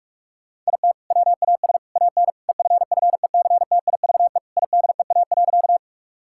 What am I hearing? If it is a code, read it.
Morse code, heard as a keyed tone, about 41 wpm, 700 Hz. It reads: IT WAS AN EFFECTIVE IDEA.
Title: Sentences from Top 1000 Words - Review Part 6 - 45wpm